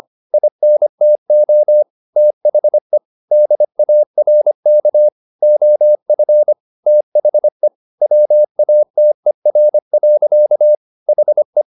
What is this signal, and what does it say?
Morse code, heard as a keyed tone, about 25 wpm, 600 Hz. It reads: INTO THE DARK OF THE WATER. HE
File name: the old man and the sea part 1 25wpm